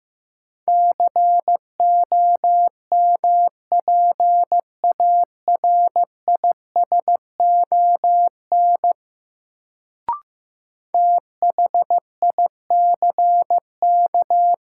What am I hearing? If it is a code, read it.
Morse code, heard as a keyed tone, about 15 wpm, 700 Hz. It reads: COMPARISON THICK
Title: Sentences from Top 1600 Words - New Word Drill Alt - 15wpm